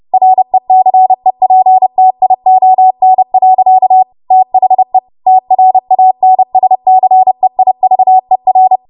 Morse code, heard as a keyed tone, about 30 wpm, 750 Hz. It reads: RECEPTION. THE TRANSCEIVER